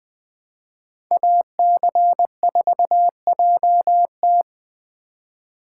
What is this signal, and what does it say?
Morse code, heard as a keyed tone, about 23 wpm, 700 Hz. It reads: AC4JT
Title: Speed Racing - USA 2x2 Call Signs 20wpm